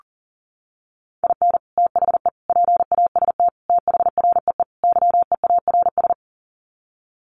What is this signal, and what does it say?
Morse code, heard as a keyed tone, about 40 wpm, 700 Hz. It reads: IN THE PAST THREE YEARS
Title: Sets of 5 Words Encore2 40wpm